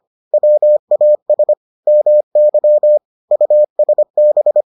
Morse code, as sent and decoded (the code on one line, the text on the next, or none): .-- .- ... / -- -.-- / ..- ... -...
WAS MY USB